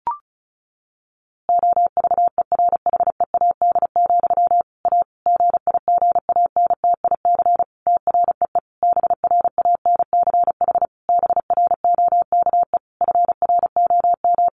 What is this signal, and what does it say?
Morse code, heard as a keyed tone, about 34 wpm, 700 Hz. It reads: OVERHEAD, A GIGANTIC TREE BRANCH BROKE FROM
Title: A Sound Of Thunder 35wpm